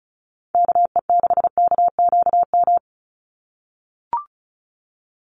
Morse code, heard as a keyed tone, about 34 wpm, 700 Hz. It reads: KE6KQM
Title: USA Call Signs 35wpm